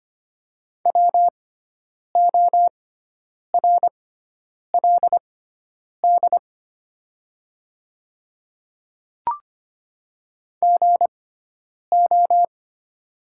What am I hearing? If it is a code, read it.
Morse code, heard as a keyed tone, about 25 wpm, spaced as for 9 wpm, 700 Hz. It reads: WORLD GO
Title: 6x Character Spacing - Top 200 Words - 25wpm